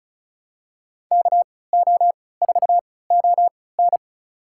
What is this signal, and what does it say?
Morse code, heard as a keyed tone, about 36 wpm, spaced as for 20 wpm, 700 Hz. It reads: KO4ON
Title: USA Call Signs - 3x Character Spacing - 35wpm